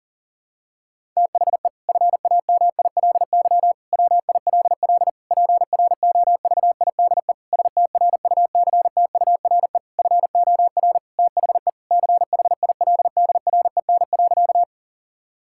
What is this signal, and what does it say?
Morse code, heard as a keyed tone, about 40 wpm, 700 Hz. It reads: THE FAMILY WILL PROVIDE STRUCTURE FOR THE CHILDREN.
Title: Sentences from Top 1000 Words - Review Part 6 - 40wpm